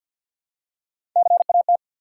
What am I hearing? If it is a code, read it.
Morse code, heard as a keyed tone, about 47 wpm, 700 Hz. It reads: CAT
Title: Three Letter Words - 50wpm